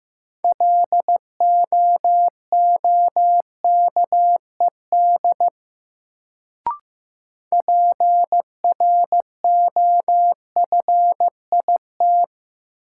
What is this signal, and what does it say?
Morse code, heard as a keyed tone, about 15 wpm, 700 Hz. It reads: LOOKED PROFIT
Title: Sentences from Top 800 Words - New Word Drill - 15wpm